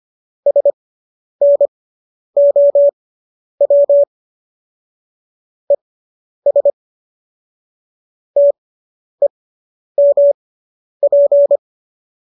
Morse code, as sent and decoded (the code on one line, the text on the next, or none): ... -. --- .-- / . ... / - . -- .--.
SNOW ES TEMP